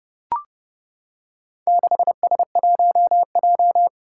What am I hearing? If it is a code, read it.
Morse code, heard as a keyed tone, about 30 wpm, 700 Hz. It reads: 6S1J
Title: Four Challenging Letter-Number Combinations 30wpm